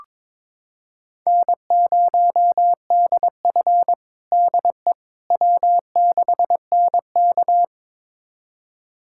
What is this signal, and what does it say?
Morse code, heard as a keyed tone, about 21 wpm, 700 Hz. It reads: N0DF DE W6NK
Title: USA Callsign DE Callsign - 4 Characters - 22wpm